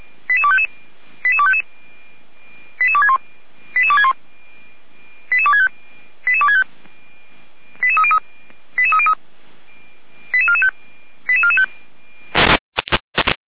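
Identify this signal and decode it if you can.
Five-tone selective call (ZVEI), heard as quick runs of five tones, tones of about 70 ms. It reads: 88255 88255 88261 88261 88265 88265 88303 88303 88445 88445
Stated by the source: ZVEI five-tone selective call received by SDR.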